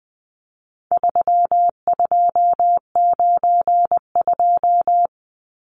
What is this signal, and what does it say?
Morse code, heard as a keyed tone, about 20 wpm, 700 Hz. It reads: 3292
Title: Numbers - 4-Digit 20wpm